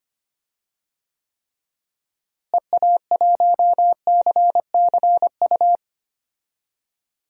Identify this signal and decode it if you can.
Morse code, heard as a keyed tone, about 25 wpm, 700 Hz. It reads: EA1CCU
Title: DXCC-100 Call Signs - 6 Characters - Repeated 3x - 25wpm